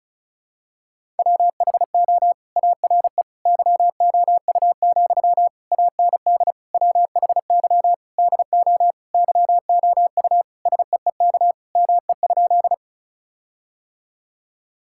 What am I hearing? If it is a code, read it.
Morse code, heard as a keyed tone, about 36 wpm, 700 Hz. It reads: WHO ARE YOU, AND WHY DO YOU SEEK ME?
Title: Sentences from Top 1600 Words 35wpm